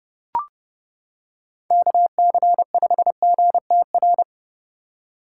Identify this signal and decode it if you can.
Morse code, heard as a keyed tone, about 30 wpm, 700 Hz. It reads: KC5GTR